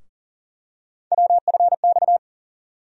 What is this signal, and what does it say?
Morse code, heard as a keyed tone, about 40 wpm, 700 Hz. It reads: WFX
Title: USA Call Sign Postfixes - 3 Letters 40wpm